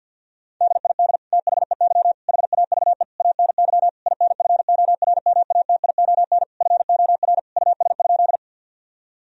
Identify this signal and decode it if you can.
Morse code, heard as a keyed tone, about 50 wpm, 700 Hz. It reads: DID THEY HAVE ANY INFORMATION FOR US?
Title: Sentences from Top 1000 Words - Review Part 3 - 50wpm